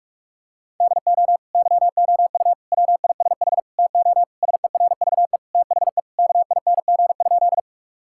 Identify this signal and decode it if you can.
Morse code, heard as a keyed tone, about 41 wpm, 700 Hz. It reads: DO YOU WISH TO SERVE THE KING?
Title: Sentences from Top 1600 Words 45wpm